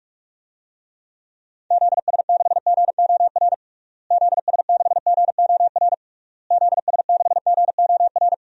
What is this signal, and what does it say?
Morse code, heard as a keyed tone, about 46 wpm, 700 Hz. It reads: ZS6GOR ZS6GOR ZS6GOR
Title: DXCC-100 Call Signs - 6 Characters - Repeated 3x - 45wpm